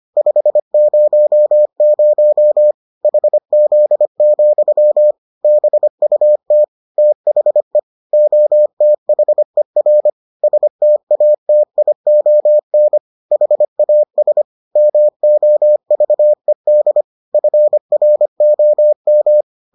Morse code, heard as a keyed tone, about 25 wpm, 600 Hz. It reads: 500 HZ, BUT THE OTHER STATION HAS MOVED FROM